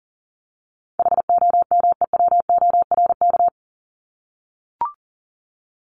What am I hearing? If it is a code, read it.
Morse code, heard as a keyed tone, about 40 wpm, 700 Hz. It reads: HOMEWORK
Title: Top 1501-1750 Words 40wpm